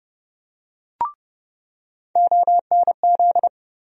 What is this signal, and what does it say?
Morse code, heard as a keyed tone, about 32 wpm, 700 Hz. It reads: ONZ
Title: USA Call Sign Postfixes - 3 Letters 30wpm